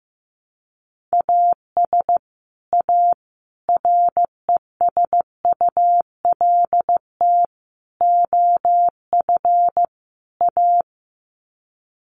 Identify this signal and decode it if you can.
Morse code, heard as a keyed tone, about 15 wpm, 700 Hz. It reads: AS A RESULT OF A
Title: Sets of 5 Words Encore 15wpm